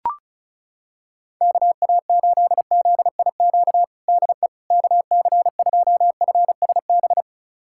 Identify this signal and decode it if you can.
Morse code, heard as a keyed tone, about 36 wpm, 700 Hz. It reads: KA8ZIQ DE KC2FSB
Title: USA Callsign DE Callsign 35wpm